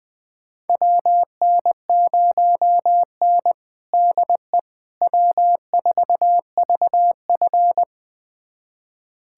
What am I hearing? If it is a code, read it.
Morse code, heard as a keyed tone, about 20 wpm, 700 Hz. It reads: WN0N DE W4VF
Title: USA Callsign DE Callsign - 4 Characters - 20wpm